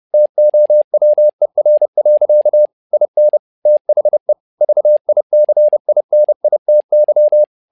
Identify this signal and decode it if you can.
Morse code, heard as a keyed tone, about 30 wpm, 600 Hz. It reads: TOWER. IN THE VICINITY
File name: learning center / tower trouble / tower trouble 30wpm